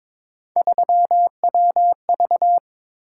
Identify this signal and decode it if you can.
Morse code, heard as a keyed tone, about 21 wpm, 700 Hz. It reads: 3WV
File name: practice / 22wpm / Three Letter-Number Combinations - 22wpm